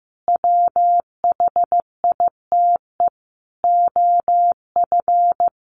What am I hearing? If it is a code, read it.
Morse code, heard as a keyed tone, about 15 wpm, 700 Hz. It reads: WHITE OF
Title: Sentences from Top 500 Words 15wpm